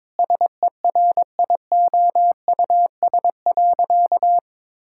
Morse code, heard as a keyed tone, about 22 wpm, 700 Hz. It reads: SERIOUS.
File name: practice / 22wpm / Sentences with 3-Word Phrases - 22wpm